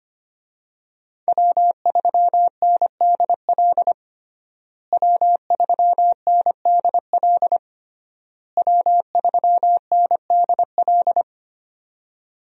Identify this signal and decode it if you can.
Morse code, heard as a keyed tone, about 25 wpm, 700 Hz. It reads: W3NDL W3NDL W3NDL
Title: USA Call Signs - 5 Characters - Repeated 3x - 25wpm